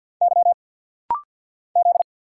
Morse code, heard as a keyed tone, about 49 wpm, 700 Hz. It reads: Y Z
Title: Single Letter-Number Alt - Rapid-Fire 50wpm